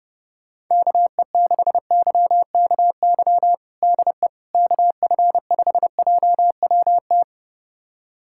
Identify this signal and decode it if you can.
Morse code, heard as a keyed tone, about 32 wpm, 700 Hz. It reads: KE6YKY DE KF5JWT
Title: USA Callsign DE Callsign - 6 Characters - 30wpm